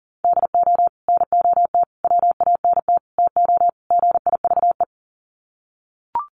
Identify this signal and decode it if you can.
Morse code, heard as a keyed tone, about 40 wpm, 700 Hz. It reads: DO NOT WANT TO GIVE
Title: Sets of 5 Words Encore 40wpm